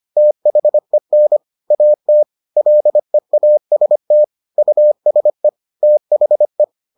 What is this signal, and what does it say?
Morse code, heard as a keyed tone, about 25 wpm, 600 Hz. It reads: THEN AT LEAST USE THE